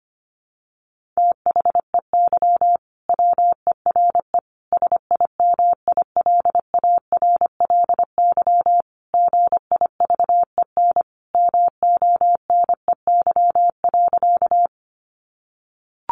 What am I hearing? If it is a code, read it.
Morse code, heard as a keyed tone, about 25 wpm, 700 Hz. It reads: THEY WERE SIMILARLY GIVEN MONEY.
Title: Sentences from Top 1400 Words 25wpm